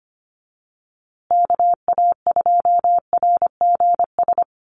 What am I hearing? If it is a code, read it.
Morse code, heard as a keyed tone, about 25 wpm, 700 Hz. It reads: KA2RGS
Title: USA Call Signs Repeated 3x 25wpm